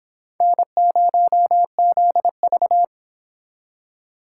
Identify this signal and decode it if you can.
Morse code, heard as a keyed tone, about 29 wpm, 700 Hz. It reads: N0ZV